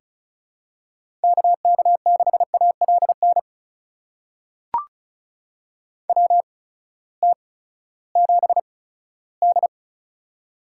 Morse code, heard as a keyed tone, about 36 wpm, spaced as for 18 wpm, 700 Hz. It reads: KK6ALN WT7D
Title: USA Call Signs - 8x Character Spacing - 35wpm